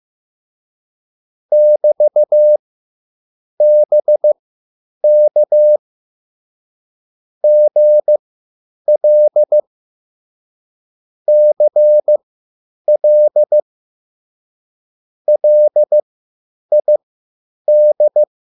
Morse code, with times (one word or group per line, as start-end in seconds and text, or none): 1.5-5.8 <BT>BK
7.4-9.6 GL
11.3-13.6 CL
15.3-18.2 LID